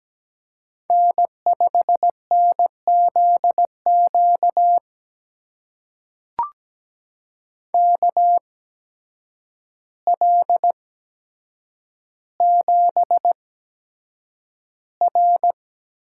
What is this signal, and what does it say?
Morse code, heard as a keyed tone, about 17 wpm, spaced as for 8 wpm, 700 Hz. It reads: N5NZQ KL7R
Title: USA Call Signs - 5 Characters - 8x Character Spacing - 17wpm